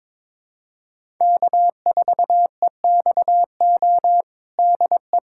Morse code, heard as a keyed tone, about 21 wpm, 700 Hz. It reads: K4EXO DE
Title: USA Callsign DE Callsign 22wpm